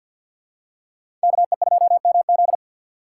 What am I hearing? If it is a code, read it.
Morse code, heard as a keyed tone, about 52 wpm, 700 Hz. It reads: KE2MZ
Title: USA Call Signs - 5 Characters - 3x Character Spacing - 50wpm